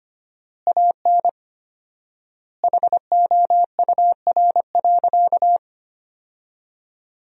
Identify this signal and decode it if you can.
Morse code, heard as a keyed tone, about 25 wpm, 700 Hz. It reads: AN HOUR.
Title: 4x Word-Spacing - Sentences from Top 500 Words - 25wpm